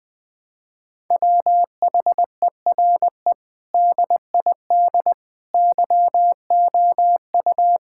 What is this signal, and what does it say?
Morse code, heard as a keyed tone, about 20 wpm, 700 Hz. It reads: WHERE DID YOU
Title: Sentences from Top 1000 Words - Review Part 6 - 20wpm